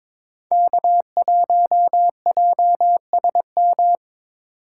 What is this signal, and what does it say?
Morse code, heard as a keyed tone, about 22 wpm, 700 Hz. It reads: K1JSM
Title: USA Call Signs - 5 Characters - 8x Character Spacing - 22wpm